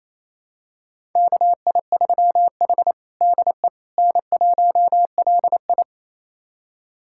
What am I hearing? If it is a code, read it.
Morse code, heard as a keyed tone, about 27 wpm, 700 Hz. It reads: KI3H DE N1LI
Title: USA Callsign DE Callsign - 4 Characters - 28wpm